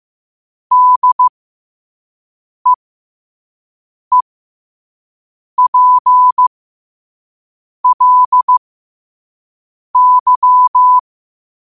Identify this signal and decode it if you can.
Morse code, heard as a keyed tone, about 15 wpm, spaced as for 5 wpm, 1000 Hz. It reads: DEEPLY
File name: 5wpm